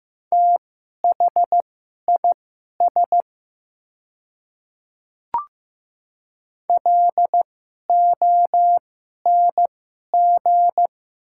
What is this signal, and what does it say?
Morse code, heard as a keyed tone, about 15 wpm, spaced as for 11 wpm, 700 Hz. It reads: THIS LONG
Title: 2x Character Spacing - Top 100 Words - 15wpm